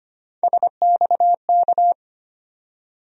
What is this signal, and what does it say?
Morse code, heard as a keyed tone, about 25 wpm, 700 Hz. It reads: SXK
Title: USA Call Sign Postfixes - 3 Letters 25wpm